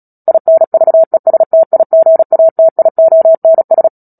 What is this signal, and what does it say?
Morse code, heard as a keyed tone, about 37 wpm, 650 Hz. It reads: INVESTIGATIONS
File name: Book Two / 10-Chapter 10 - Epilogue - 36 WPM